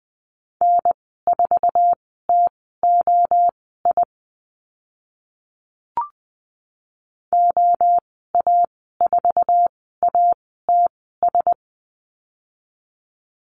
Morse code, heard as a keyed tone, about 20 wpm, spaced as for 14 wpm, 700 Hz. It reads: N4TOI OA4ATS
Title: 076 Lesson 13g - Letter L - Call Signs 20wpm